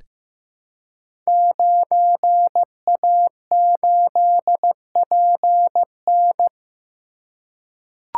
Morse code, heard as a keyed tone, about 15 wpm, 700 Hz. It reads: 9A8PN
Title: Code Groups - 5 Letter:Numbers - 15wpm